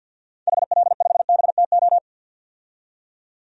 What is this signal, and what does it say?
Morse code, heard as a keyed tone, about 52 wpm, 700 Hz. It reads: HL5BTO
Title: DXCC-100 Call Signs - 6 Characters - 50wpm